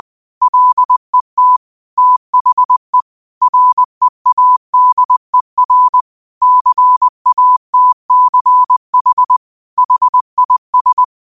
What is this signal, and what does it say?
Morse code, heard as a keyed tone, about 20 wpm, 1000 Hz. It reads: LET THE READER CATCH HIS